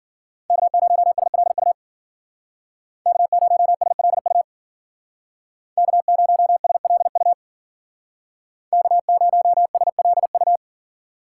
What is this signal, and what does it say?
Morse code, heard as a keyed tone, about 52 wpm, 700 Hz. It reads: K0SLU K0SLU K0SLU K0SLU